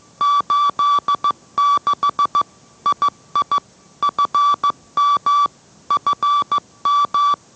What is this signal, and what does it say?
Morse code, heard as a keyed tone, about 15 wpm, 1160 Hz. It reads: 86 II FM FM